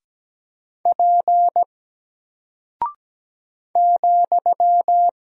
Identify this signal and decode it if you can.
Morse code, heard as a keyed tone, about 17 wpm, 700 Hz. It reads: P ,
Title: Single Character (Letter:Number:Punctuation:Prosign) 17wpm